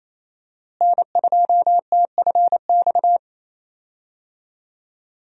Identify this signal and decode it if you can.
Morse code, heard as a keyed tone, about 28 wpm, 700 Hz. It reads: N2TFX
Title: USA Call Signs - 5 Characters - Repeated 2x - 28wpm